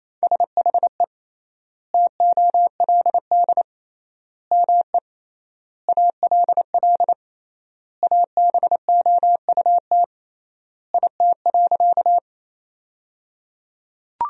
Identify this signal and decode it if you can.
Morse code, heard as a keyed tone, about 28 wpm, 700 Hz. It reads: SHE TOLD ME ALL ABOUT IT.
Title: 3x Word-Spacing - Sentences from Top 400 Words - 28wpm